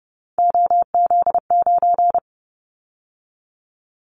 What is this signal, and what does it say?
Morse code, heard as a keyed tone, about 30 wpm, 700 Hz. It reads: OZ9
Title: Callsign Prefixes for Top 100 DXCC - 30wpm